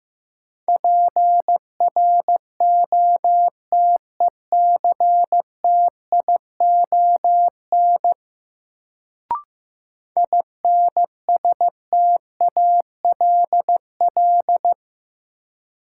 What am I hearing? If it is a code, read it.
Morse code, heard as a keyed tone, about 15 wpm, 700 Hz. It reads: PROTECTION INSTALL
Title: Sentences from Top 1600 Words - New Word Drill - 15wpm